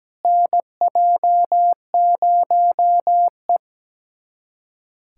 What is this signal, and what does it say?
Morse code, heard as a keyed tone, about 17 wpm, 700 Hz. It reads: NJ0E
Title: USA Callsign DE Callsign - 4 Characters - 17wpm